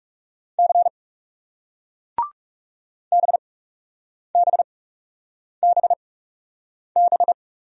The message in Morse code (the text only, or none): C BBBB